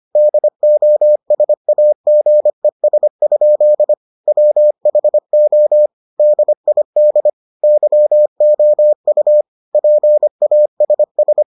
Morse code, heard as a keyed tone, about 25 wpm, 600 Hz. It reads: DOSAGES? WHO DID YOU PASS